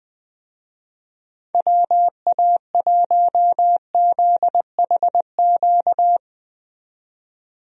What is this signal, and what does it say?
Morse code, heard as a keyed tone, about 21 wpm, 700 Hz. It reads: WA1ZHQ
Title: USA Call Signs - 6 Characters - 20wpm